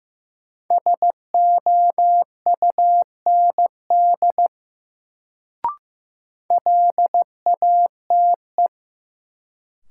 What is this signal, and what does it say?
Morse code, heard as a keyed tone, about 15 wpm, 700 Hz. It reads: SOUND LATE
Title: Sets of 3 Words - New Word Drill - 15wpm